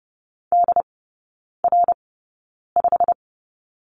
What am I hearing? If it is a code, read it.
Morse code, heard as a keyed tone, about 30 wpm, spaced as for 9 wpm, 700 Hz. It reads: DR5
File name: course / 30:10wpm / 087 Lesson 15e - Number 5 - Three Characters 30wpm